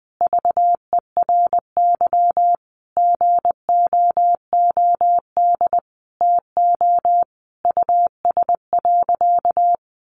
Morse code, heard as a keyed tone, about 20 wpm, 700 Hz. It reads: VERY GOOD TO US.
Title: Sentences from Top 200 Words 20wpm